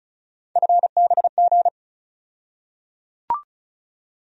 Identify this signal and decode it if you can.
Morse code, heard as a keyed tone, about 36 wpm, 700 Hz. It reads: FBG
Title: USA Call Sign Postfixes - 3 Letters 35wpm